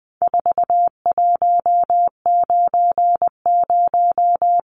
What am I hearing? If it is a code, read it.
Morse code, heard as a keyed tone, about 20 wpm, 700 Hz. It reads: 4190
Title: Numbers - 4-Digit 20wpm